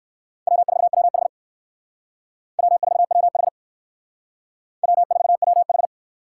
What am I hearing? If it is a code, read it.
Morse code, heard as a keyed tone, about 59 wpm, 700 Hz. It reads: W4WH W4WH W4WH